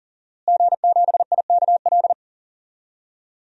Morse code, heard as a keyed tone, about 40 wpm, 700 Hz. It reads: G7IKL
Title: DXCC-100 Call Signs - 5 Characters - Repeated 2x - 40wpm